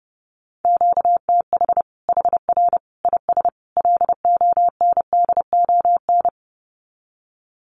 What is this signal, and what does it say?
Morse code, heard as a keyed tone, about 28 wpm, 700 Hz. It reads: QTH HR IS LONDON
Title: Common QSO Phrases 30wpm